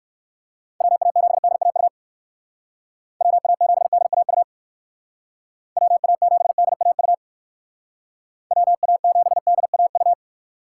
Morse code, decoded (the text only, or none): WA7DAU WA7DAU WA7DAU WA7DAU